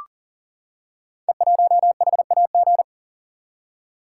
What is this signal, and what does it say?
Morse code, heard as a keyed tone, about 40 wpm, 700 Hz. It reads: E1HAG
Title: Code Groups - 5 Letter:Numbers - 40wpm